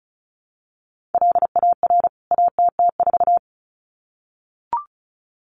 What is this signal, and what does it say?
Morse code, heard as a keyed tone, about 34 wpm, 700 Hz. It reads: LAR ATT4
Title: CWT Exchange 35wpm